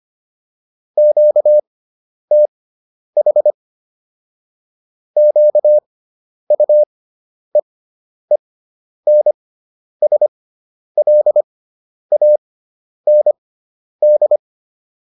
Morse code, heard as a keyed tone, about 25 wpm, spaced as for 10 wpm, 600 Hz. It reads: QTH QUEENSLAND
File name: QSO407 10